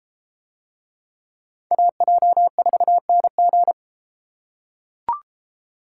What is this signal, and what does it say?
Morse code, heard as a keyed tone, about 33 wpm, 700 Hz. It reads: AJ4NG